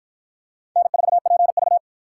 Morse code, heard as a keyed tone, about 53 wpm, 700 Hz. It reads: N4PV